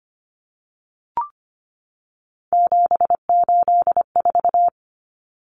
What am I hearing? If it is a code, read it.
Morse code, heard as a keyed tone, about 25 wpm, 700 Hz. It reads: 784
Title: Common Cut Numbers - 3-Digit 25wpm